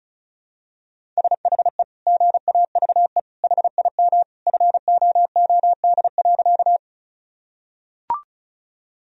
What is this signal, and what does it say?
Morse code, heard as a keyed tone, about 34 wpm, 700 Hz. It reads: SHE GAVE HIM FOOD.
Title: Sentences from Top 1000 Words - Review Part 1 - 35wpm